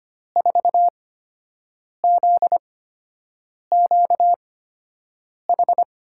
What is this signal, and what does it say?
Morse code, heard as a keyed tone, about 25 wpm, spaced as for 7 wpm, 700 Hz. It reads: 4ZQH
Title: USA Call Signs - 6 Characters - 8x Character Spacing - 25wpm